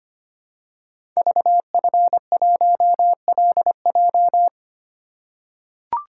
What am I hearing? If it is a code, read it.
Morse code, heard as a keyed tone, about 25 wpm, 700 Hz. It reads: VF1LJ
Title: Code Groups - 5 Letter:Numbers - 25wpm